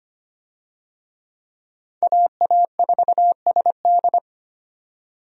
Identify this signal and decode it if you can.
Morse code, heard as a keyed tone, about 25 wpm, 700 Hz. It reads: AA4SD
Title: USA Call Signs - 5 Characters - 25wpm